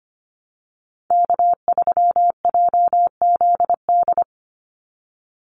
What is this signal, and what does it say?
Morse code, heard as a keyed tone, about 25 wpm, 700 Hz. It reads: K3JZD
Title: NA SOTA Chaser Callsigns 25wpm